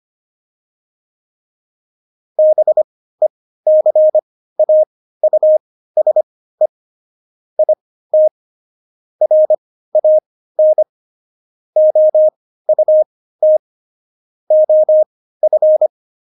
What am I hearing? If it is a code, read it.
Morse code, heard as a keyed tone, about 25 wpm, spaced as for 15 wpm, 620 Hz. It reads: BECAUSE IT RAN OUT OF
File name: Jokes-25-char-15-eff